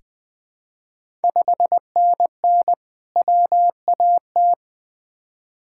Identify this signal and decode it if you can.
Morse code, heard as a keyed tone, about 20 wpm, 700 Hz. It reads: 5NN WAT
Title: Ontario QSO Party Exchange 20wpm